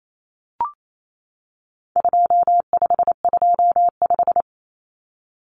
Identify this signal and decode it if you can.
Morse code, heard as a keyed tone, about 28 wpm, 700 Hz. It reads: 2525
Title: Common Cut Numbers - 4-Digit 28wpm